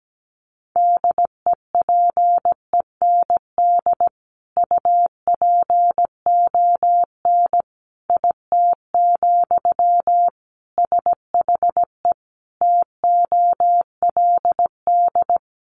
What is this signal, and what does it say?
Morse code, heard as a keyed tone, about 17 wpm, 700 Hz. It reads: DEPEND UPON IT, SHE TOLD
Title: Sentences from Top 1300 Words 17wpm